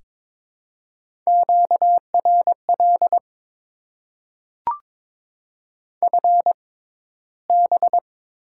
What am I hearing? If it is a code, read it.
Morse code, heard as a keyed tone, about 22 wpm, spaced as for 14 wpm, 700 Hz. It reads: QRL FB
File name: Top 20 QSO Elements - 6x Character Spacing - 22wpm